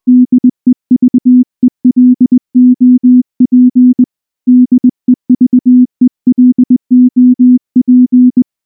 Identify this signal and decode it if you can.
Morse code, heard as a keyed tone, about 20 wpm, 250 Hz. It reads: DEVELOP DEVELOP
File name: morse / morse-training-20WPM-3